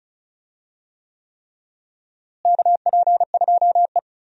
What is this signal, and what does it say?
Morse code, heard as a keyed tone, about 34 wpm, 700 Hz. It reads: KP2E
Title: DXCC-100 Call Signs - 4 Characters - Repeated 3x - 35wpm